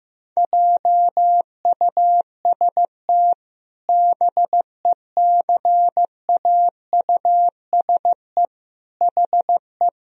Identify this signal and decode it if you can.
Morse code, heard as a keyed tone, about 15 wpm, 700 Hz. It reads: JUST BECAUSE HE
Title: Sentences from Top 1000 Words - Review Part 3 - 15wpm